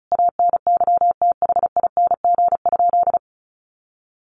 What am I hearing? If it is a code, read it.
Morse code, heard as a keyed tone, about 34 wpm, 700 Hz. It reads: ANYTHING?
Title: Sentences from Top 1000 Words 35wpm